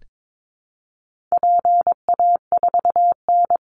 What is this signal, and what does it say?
Morse code, heard as a keyed tone, about 22 wpm, 700 Hz. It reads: PA4N
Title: Call Signs Repeated 2x - 22wpm